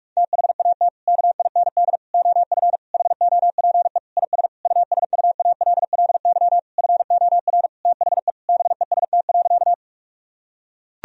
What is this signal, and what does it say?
Morse code, heard as a keyed tone, about 41 wpm, 700 Hz. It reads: THAT KIND OF HOPE IS USUALLY FOR THE BEST.
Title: Sentences from Top 1000 Words - Review Part 3 - 45wpm